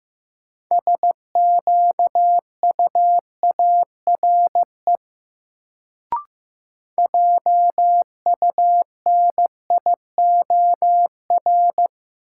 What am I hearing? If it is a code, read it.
Morse code, heard as a keyed tone, about 15 wpm, 700 Hz. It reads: SQUARE JUNIOR
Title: Sentences from Top 1000 Words - Review Part 7 - New Word Drill - 15wpm